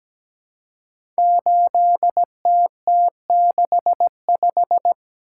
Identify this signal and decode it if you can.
Morse code, heard as a keyed tone, about 17 wpm, 700 Hz. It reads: 8TT65
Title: Code Groups - 5 Letter:Numbers - 17wpm